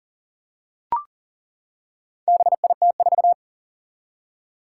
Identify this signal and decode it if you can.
Morse code, heard as a keyed tone, about 38 wpm, 700 Hz. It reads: BIT4
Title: Four Challenging Letter-Number Combinations 40wpm